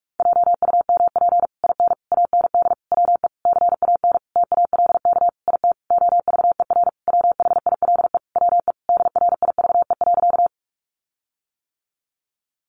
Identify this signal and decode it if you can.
Morse code, heard as a keyed tone, about 44 wpm, 700 Hz. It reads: JUMP IN AND WE CAN TALK IT OVER WHILE WE DRIVE.
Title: Sentences from Top 1300 Words 45wpm